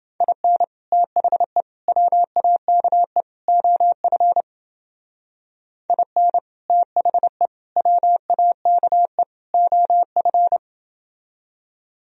Morse code, heard as a keyed tone, about 30 wpm, 700 Hz. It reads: IN THE WAKE OF IN THE WAKE OF